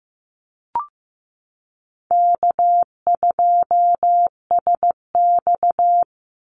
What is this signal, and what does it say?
Morse code, heard as a keyed tone, about 15 wpm, 700 Hz. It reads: K2SX
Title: Call Signs Repeated 3x - 15wpm